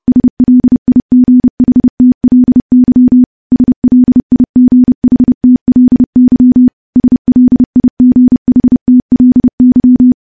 Morse code, heard as a keyed tone, about 30 wpm, 250 Hz. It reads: SLIGHTLY SLIGHTLY SLIGHTLY
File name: morse / morse-training-30WPM